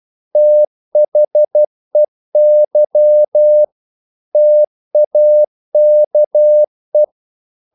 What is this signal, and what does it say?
Morse code, heard as a keyed tone, about 12 wpm, 600 Hz. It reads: THEY TAKE
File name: CW / Practice / EnglishQuotations0312